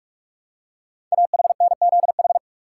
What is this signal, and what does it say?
Morse code, heard as a keyed tone, about 44 wpm, 700 Hz. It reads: AHNZH
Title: Code Groups - 5 Letters - 45wpm